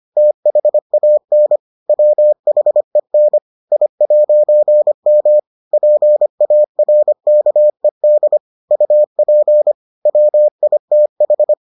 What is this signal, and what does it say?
Morse code, heard as a keyed tone, about 25 wpm, 600 Hz. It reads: THAN WHEN I'M PARKED UP WITH